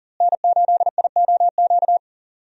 Morse code, heard as a keyed tone, about 37 wpm, 700 Hz. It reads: N8IOQ
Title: USA Call Signs - 5 Characters - Repeated 2x - 40wpm